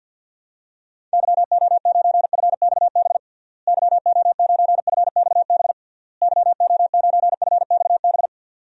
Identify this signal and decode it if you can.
Morse code, heard as a keyed tone, about 48 wpm, 700 Hz. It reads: YO9FXB YO9FXB YO9FXB
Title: DXCC-100 Call Signs - 6 Characters - Repeated 3x - 50wpm